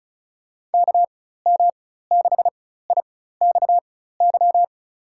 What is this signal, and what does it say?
Morse code, heard as a keyed tone, about 34 wpm, spaced as for 16 wpm, 700 Hz. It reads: KM6IXY
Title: USA Call Signs - 4x Character Spacing - 35wpm